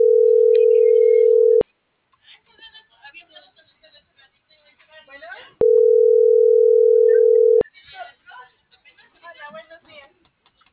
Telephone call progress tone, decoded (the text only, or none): ringback tone